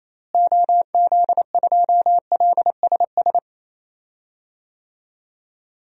Morse code, heard as a keyed tone, about 28 wpm, 700 Hz. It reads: OZ2LSS
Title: DXCC-100 Call Signs - 6 Characters - Repeated 3x - 28wpm